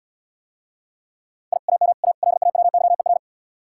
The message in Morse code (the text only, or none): I AM A LAWYER